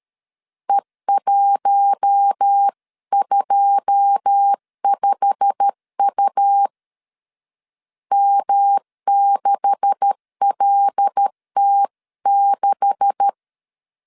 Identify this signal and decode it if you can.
Morse code, heard as a keyed tone, about 13 wpm, 800 Hz. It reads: E125U M6LT6
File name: CW / DET / 1 9 1500C S B 12wpm 3819